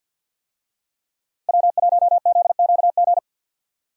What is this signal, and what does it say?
Morse code, heard as a keyed tone, about 52 wpm, 700 Hz. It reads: W1ZQG
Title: USA Call Signs - 5 Characters - Repeated 3x - 50wpm